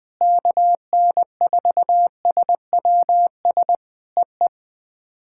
Morse code, heard as a keyed tone, about 20 wpm, 700 Hz. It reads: KN4SWS EE
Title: POTA QSO Practice - 20wpm